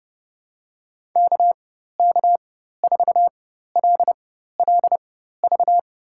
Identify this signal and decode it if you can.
Morse code, heard as a keyed tone, about 30 wpm, spaced as for 14 wpm, 700 Hz. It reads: KK4LLV
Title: USA Call Signs - 4x Character Spacing - 30wpm